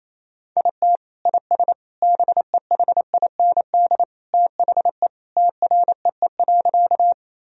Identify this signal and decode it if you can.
Morse code, heard as a keyed tone, about 28 wpm, 700 Hz. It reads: IT IS BEHIND THE TREE.
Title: Sentences from Top 1000 Words - Review Part 1 - 28wpm